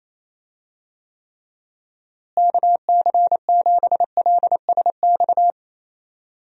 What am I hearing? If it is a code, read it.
Morse code, heard as a keyed tone, about 28 wpm, 700 Hz. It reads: KC7LSX
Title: USA Call Signs - 6 Characters - Repeated 3x - 28wpm